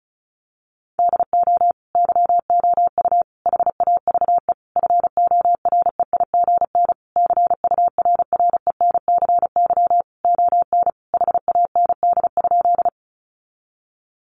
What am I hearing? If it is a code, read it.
Morse code, heard as a keyed tone, about 34 wpm, 700 Hz. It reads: DO YOU HAVE FOREIGN CURRENCY ON HAND?
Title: Sentences from Top 1500 Words 35wpm